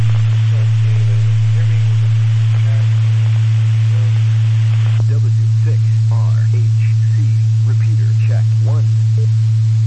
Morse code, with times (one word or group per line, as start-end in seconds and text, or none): none